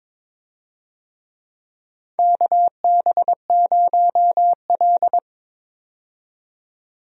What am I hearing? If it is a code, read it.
Morse code, heard as a keyed tone, about 22 wpm, 700 Hz. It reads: KB0L
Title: USA Call Signs - 4 Characters - 22wpm